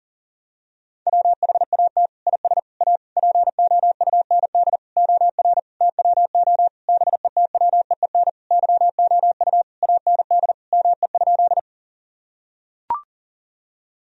Morse code, heard as a keyed tone, about 40 wpm, 700 Hz. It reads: WHAT IS A POUND OR TWO BETWEEN YOU AND ME?
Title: Sentences from Top 1000 Words - Review Part 2 - 40wpm